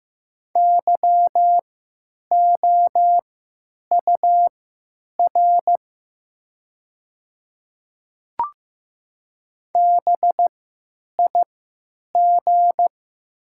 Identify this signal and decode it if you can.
Morse code, heard as a keyed tone, about 15 wpm, spaced as for 9 wpm, 700 Hz. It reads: YOUR BIG
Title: 3x Character Spacing - Top 200 Words - 15wpm